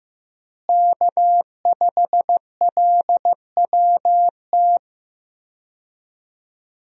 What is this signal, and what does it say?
Morse code, heard as a keyed tone, about 15 wpm, 700 Hz. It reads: K5LWT
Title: USA Call Signs - 5 Characters - Repeated 3x - 15wpm